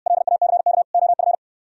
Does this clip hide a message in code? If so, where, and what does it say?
Yes: Morse code from 0.1 to 1.4 s, heard as a keyed tone, about 67 wpm, 700 Hz. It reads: LACK OF